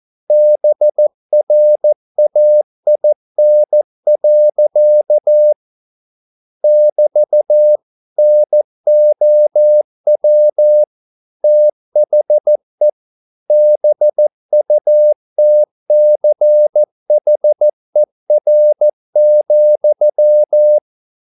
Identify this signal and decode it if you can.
Morse code, heard as a keyed tone, about 14 wpm, 600 Hz. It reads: BRAIN. <BT> NOW THE BUTCHER,
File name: mp3 files / a dogs mistake - 14wpm